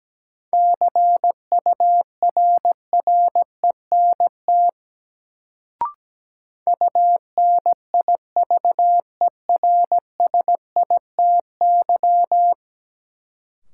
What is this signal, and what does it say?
Morse code, heard as a keyed tone, about 17 wpm, 700 Hz. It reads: CURRENT UNIVERSITY
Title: Sentences from Top 1000 Words - Review Part 4 - New Word Drill - 17wpm